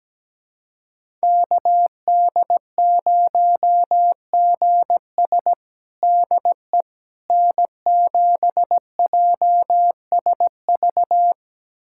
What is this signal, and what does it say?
Morse code, heard as a keyed tone, about 17 wpm, 700 Hz. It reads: KD0GS DE N7JSV
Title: USA Callsign DE Callsign - 5 Characters - 17wpm